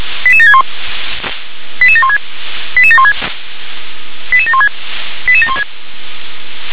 Five-tone selective call (ZVEI), heard as quick runs of five tones, tones of about 70 ms. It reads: 88761 88716 88716 88716 88716
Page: SDR/ZVEI